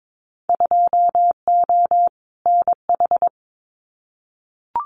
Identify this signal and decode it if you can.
Morse code, heard as a keyed tone, about 22 wpm, 700 Hz. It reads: 2O NH
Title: Winter Field Day Exchanges 22wpm